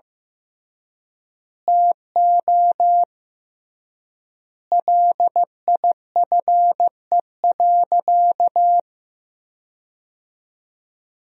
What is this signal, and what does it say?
Morse code, heard as a keyed tone, about 15 wpm, 700 Hz. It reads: TO LIFE.
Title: 3x Word-Spacing - Sentences from Top 400 Words - 15wpm